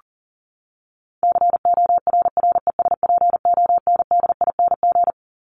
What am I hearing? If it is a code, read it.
Morse code, heard as a keyed tone, about 40 wpm, 700 Hz. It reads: CORRESPONDING
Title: ING Suffix - Top 100 Words with Base Words 40wpm